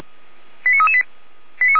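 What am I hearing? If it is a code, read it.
Five-tone selective call (ZVEI), heard as quick runs of five tones, tones of about 70 ms. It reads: 89338
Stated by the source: ZVEI five-tone selective call received via SDR.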